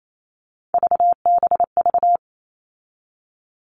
Morse code, heard as a keyed tone, about 28 wpm, 700 Hz. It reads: VBV
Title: All Cut Numbers - 3-Digit 28wpm